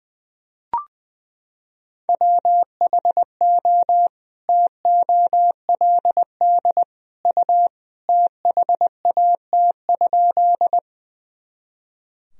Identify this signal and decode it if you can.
Morse code, heard as a keyed tone, about 21 wpm, 700 Hz. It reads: WHO TOLD U THAT?
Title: Sentences from Top 400 Words and Abbreviations 20wpm